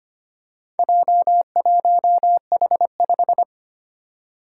Morse code, heard as a keyed tone, about 25 wpm, 700 Hz. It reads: J1H5
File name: practice / 25wpm / Four Letter-Number Combinations - 25wpm